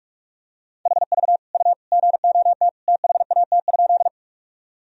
Morse code, heard as a keyed tone, about 46 wpm, 700 Hz. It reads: HV U GOT THAT?
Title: Sentences from Top 300 Words and Abbreviations 45wpm